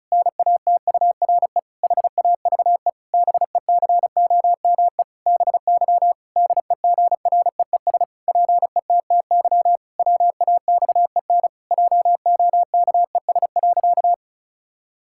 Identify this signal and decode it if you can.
Morse code, heard as a keyed tone, about 34 wpm, 700 Hz. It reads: NATURE HAVE BECOME BY DEGREES PETTY WAXEN JOKES.
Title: Pangram Challenge v2 - 35wpm